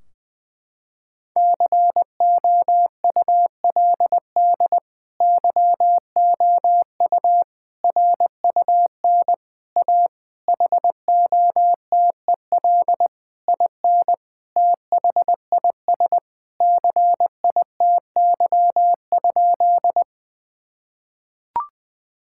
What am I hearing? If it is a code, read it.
Morse code, heard as a keyed tone, about 21 wpm, 700 Hz. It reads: COULD YOU RUN A HOTEL IN THIS CITY?
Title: Sentences from Top 1600 Words 20wpm